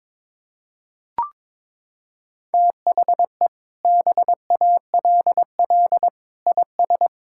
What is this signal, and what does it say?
Morse code, heard as a keyed tone, about 22 wpm, 700 Hz. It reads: THE BALL IS
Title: Sentences from Top 1000 Words - Review Part 6 - 22wpm